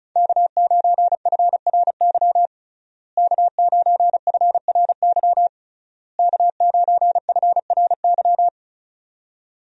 Morse code, heard as a keyed tone, about 34 wpm, 700 Hz. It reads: K9FRY K9FRY K9FRY
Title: USA Call Signs - 5 Characters - Repeated 3x - 35wpm